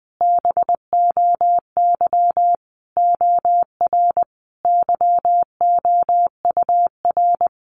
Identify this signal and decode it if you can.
Morse code, heard as a keyed tone, about 19 wpm, 700 Hz. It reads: BOY OR YOUR
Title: Sentences from Top 1500 Words 20wpm